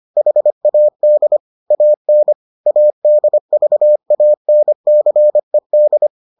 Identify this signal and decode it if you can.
Morse code, heard as a keyed tone, about 25 wpm, 600 Hz. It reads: HAD AN ADVANCED